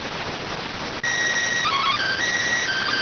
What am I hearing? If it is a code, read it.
SSTV (slow-scan television), PD 290.
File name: AXIAL fr130 3mm 409.gen-05 rx